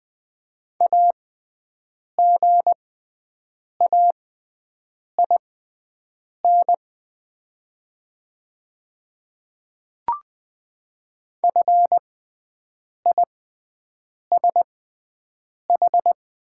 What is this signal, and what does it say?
Morse code, heard as a keyed tone, about 20 wpm, spaced as for 7 wpm, 700 Hz. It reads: AGAIN FISH
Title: 6x Character Spacing - Top 300 Words - 20wpm